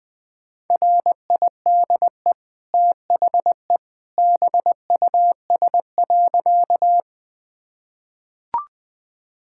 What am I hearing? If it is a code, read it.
Morse code, heard as a keyed tone, about 20 wpm, 700 Hz. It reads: RIDE THE BUS.
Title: Easy Sentences with Hints - 20wpm